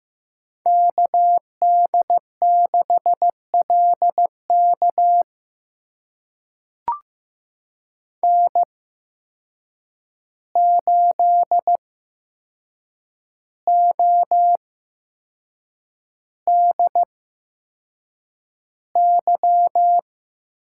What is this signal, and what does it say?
Morse code, heard as a keyed tone, about 15 wpm, spaced as for 6 wpm, 700 Hz. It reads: KD6LK N8ODY
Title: USA Call Signs - 5 Characters - 8x Character Spacing - 15wpm